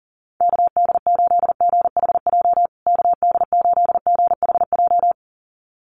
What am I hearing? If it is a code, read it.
Morse code, heard as a keyed tone, about 40 wpm, 700 Hz. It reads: KD8GHJ KD8GHJ
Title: USA Call Signs Repeated 2x 40wpm